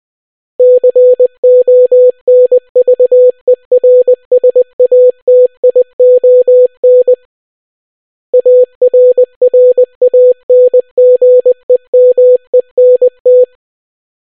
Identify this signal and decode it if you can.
Morse code, heard as a keyed tone, about 20 wpm, 500 Hz. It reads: CONVERSATION ARRANGEMENT